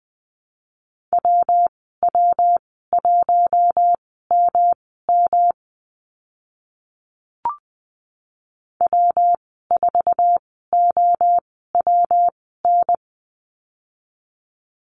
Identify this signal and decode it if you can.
Morse code, heard as a keyed tone, about 20 wpm, spaced as for 14 wpm, 700 Hz. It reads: WW1MM W4OWN
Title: 154 Lesson 25g - Letter p - Call Signs 20wpm